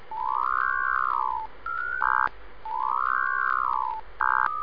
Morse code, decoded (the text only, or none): none